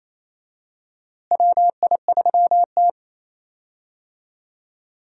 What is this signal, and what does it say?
Morse code, heard as a keyed tone, about 28 wpm, 700 Hz. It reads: WI3T
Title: USA Call Signs - 4 Characters - Repeated 3x - 28wpm